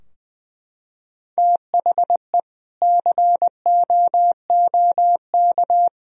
Morse code, heard as a keyed tone, about 20 wpm, 700 Hz. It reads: THE COOK
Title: Sentences from Top 1000 Words - Review Part 4 - 20wpm